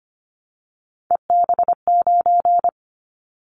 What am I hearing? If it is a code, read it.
Morse code, heard as a keyed tone, about 25 wpm, 700 Hz. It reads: EB9
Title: Callsign Prefixes for Top 100 DXCC - 25wpm